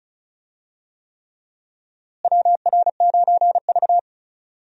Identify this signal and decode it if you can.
Morse code, heard as a keyed tone, about 34 wpm, 700 Hz. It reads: WR9V
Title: USA Call Signs - 4 Characters - Repeated 3x - 35wpm